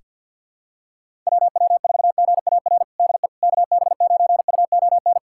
Morse code, heard as a keyed tone, about 48 wpm, 700 Hz. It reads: WW4GAR DE KB9UON
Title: USA Callsign DE Callsign - 6 Characters - 50wpm